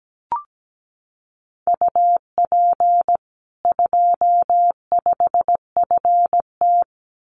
Morse code, heard as a keyed tone, about 17 wpm, 700 Hz. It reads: UP 25FT
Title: Common QSO Phrases 17wpm